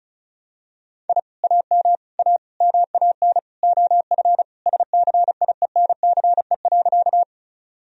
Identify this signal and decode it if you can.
Morse code, heard as a keyed tone, about 36 wpm, 700 Hz. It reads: I AM A MAN OF SCIENCE.
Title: Sentences from Top 1000 Words - Review Part 2 - 35wpm